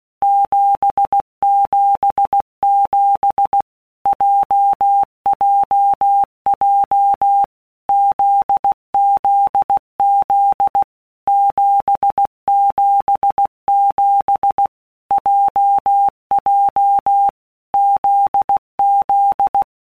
Morse code, heard as a keyed tone, about 16 wpm, 800 Hz. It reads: 777 JJJ ZZZ 777 JJ ZZ
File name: Basic Letter Groups By Blocks / Groups@15 / JZ17@15